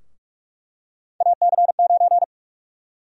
Morse code, heard as a keyed tone, about 42 wpm, 700 Hz. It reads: AC9
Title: USA Call Sign Prefixes - Repeated 2x - 45wpm